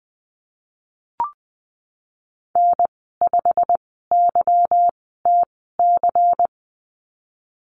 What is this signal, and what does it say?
Morse code, heard as a keyed tone, about 20 wpm, spaced as for 15 wpm, 700 Hz. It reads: N5YTC
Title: 193 Lesson 31g - Letter v - Call Signs 20wpm